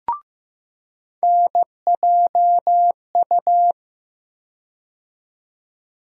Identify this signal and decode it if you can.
Morse code, heard as a keyed tone, about 16 wpm, 700 Hz. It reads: NJU
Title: USA Call Sign Postfixes - 3 Letters 15wpm